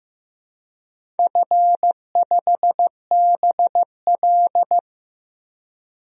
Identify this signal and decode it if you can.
Morse code, heard as a keyed tone, about 15 wpm, 700 Hz. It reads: F5BL
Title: DXCC-100 Call Signs - 4 Characters - 15wpm